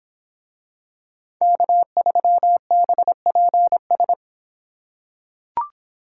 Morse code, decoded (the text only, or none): K3BPS